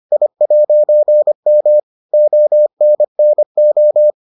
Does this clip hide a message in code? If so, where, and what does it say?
Yes: Morse code from 0.1 to 4.1 s, heard as a keyed tone, about 25 wpm, 600 Hz. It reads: I'M ONNO